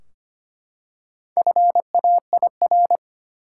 Morse code, heard as a keyed tone, about 25 wpm, 700 Hz. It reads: FAIR
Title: Four Letter Words - 25wpm